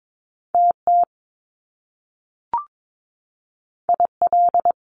Morse code, heard as a keyed tone, about 22 wpm, 700 Hz. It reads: TT IL
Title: Top Two Letter Combinations in Words 22wpm